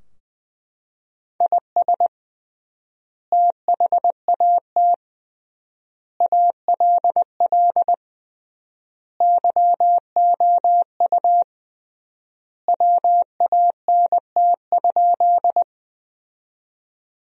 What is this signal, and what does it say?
Morse code, heard as a keyed tone, about 20 wpm, 700 Hz. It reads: IS THAT ALL YOU WANT?
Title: 3x Word-Spacing - Sentences from Top 200 Words - 20wpm